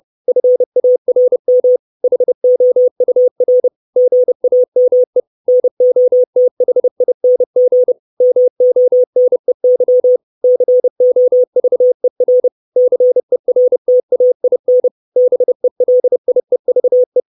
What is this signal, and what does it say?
Morse code, heard as a keyed tone, about 30 wpm, 500 Hz. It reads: FARM HOUR GAME NOTHING MONEY COVER CERTAIN BELIEVE
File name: 632 Words 30WPM